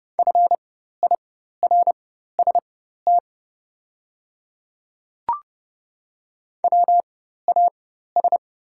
Morse code, heard as a keyed tone, about 30 wpm, spaced as for 14 wpm, 700 Hz. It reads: FIRST WAS
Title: 4x Character Spacing - Top 100 Words - 30wpm